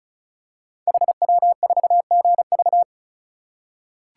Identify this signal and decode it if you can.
Morse code, heard as a keyed tone, about 36 wpm, 700 Hz. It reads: HW4GV
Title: Code Groups - 5 Letter:Numbers - 35wpm